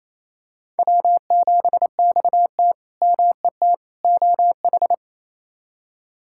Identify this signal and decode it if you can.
Morse code, heard as a keyed tone, about 28 wpm, 700 Hz. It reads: W7XT MET OH
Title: K1USN SST QSO Practice - 28wpm